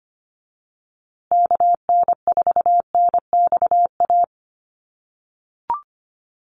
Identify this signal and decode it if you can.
Morse code, heard as a keyed tone, about 25 wpm, 700 Hz. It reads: KN4NXA
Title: USA Call Signs Repeated 3x 25wpm